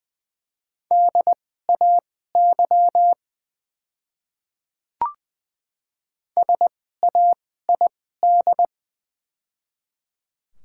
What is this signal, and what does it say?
Morse code, heard as a keyed tone, about 20 wpm, spaced as for 14 wpm, 700 Hz. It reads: DAY SAID
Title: 2x Character Spacing - Top 100 Words - 20wpm